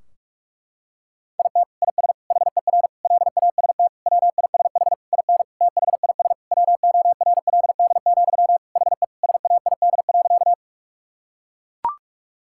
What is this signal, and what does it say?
Morse code, heard as a keyed tone, about 42 wpm, 700 Hz. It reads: IT IS HER LAST WISH IN THIS WORLD, HE SAID.
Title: Sentences from Top 1600 Words 45wpm